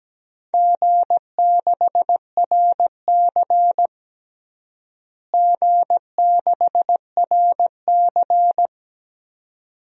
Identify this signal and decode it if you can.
Morse code, heard as a keyed tone, about 17 wpm, 700 Hz. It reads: G6RC G6RC
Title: DXCC-100 Call Signs - 4 Characters - Repeated 2x - 17wpm